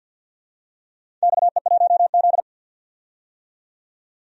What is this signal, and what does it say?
Morse code, heard as a keyed tone, about 50 wpm, 700 Hz. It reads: XE1Z
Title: DXCC-100 Call Signs - 4 Characters - 50wpm